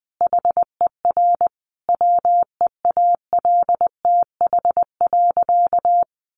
Morse code, heard as a keyed tone, about 20 wpm, 700 Hz. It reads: HER WEALTH.
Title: Sentences from Top 1500 Words 20wpm